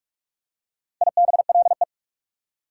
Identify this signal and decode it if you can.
Morse code, heard as a keyed tone, about 47 wpm, 690 Hz. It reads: IBLE
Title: Prefixes and Suffixes 45wpm